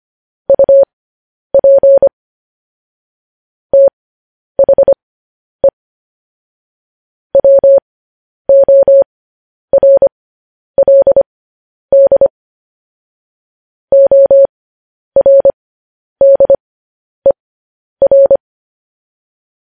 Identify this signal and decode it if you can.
Morse code, heard as a keyed tone, about 25 wpm, spaced as for 10 wpm, 550 Hz. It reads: UP THE WORLD ORDER
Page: news-headlines-in-morse-code-at-10-wpm